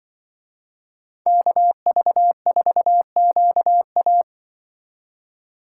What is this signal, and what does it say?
Morse code, heard as a keyed tone, about 24 wpm, 700 Hz. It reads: KV4QA
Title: Speed Racing - USA 2x2 Call Signs 20wpm